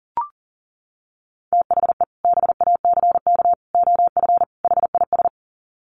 Morse code, heard as a keyed tone, about 40 wpm, 700 Hz. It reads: THE BACK OF HIS
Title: Sets of 4 Words 40wpm